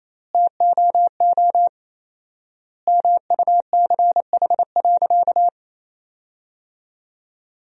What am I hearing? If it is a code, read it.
Morse code, heard as a keyed tone, about 28 wpm, 700 Hz. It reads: TOO MUCH.
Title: 4x Word-Spacing - Sentences from Top 300 Words - 28wpm